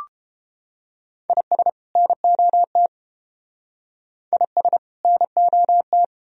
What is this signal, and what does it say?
Morse code, heard as a keyed tone, about 29 wpm, 700 Hz. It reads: IS NOT IS NOT